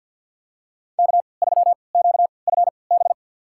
Morse code, heard as a keyed tone, about 51 wpm, spaced as for 28 wpm, 700 Hz. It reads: K3QFB
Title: USA Call Signs - 3x Character Spacing - 50wpm